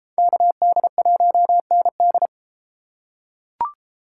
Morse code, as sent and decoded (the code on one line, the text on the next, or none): -.- -.. .---- -. -..
KD1ND